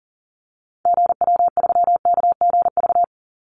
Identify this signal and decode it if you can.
Morse code, heard as a keyed tone, about 40 wpm, 700 Hz. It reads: GW3KGV
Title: Call Signs Repeated 3x - 40wpm